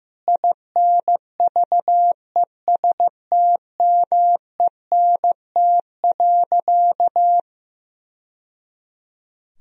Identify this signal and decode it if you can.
Morse code, heard as a keyed tone, about 15 wpm, 700 Hz. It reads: INVESTMENT.
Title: Sentences from Top 1000 Words - Review Part 4 - 15wpm